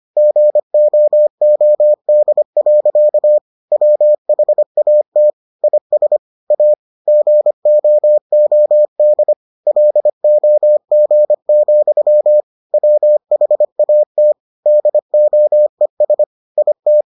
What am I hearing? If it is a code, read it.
Morse code, heard as a keyed tone, about 25 wpm, 600 Hz. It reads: GOOD. WHAT IS A GOOD LOG, WHAT DOES IT